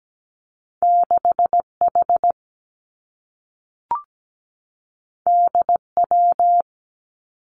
Morse code, heard as a keyed tone, about 17 wpm, 700 Hz. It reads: 6H DW
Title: Two Challenging Letter-Number Combinations 17wpm